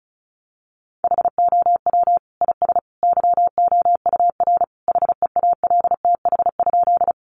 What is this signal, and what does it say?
Morse code, heard as a keyed tone, about 34 wpm, 700 Hz. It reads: HOW IS YOUR HEALTH?
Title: Sentences from Top 600 Words 35wpm